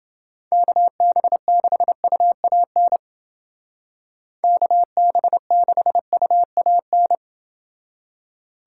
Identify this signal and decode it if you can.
Morse code, heard as a keyed tone, about 29 wpm, 700 Hz. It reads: KB6UAN KB6UAN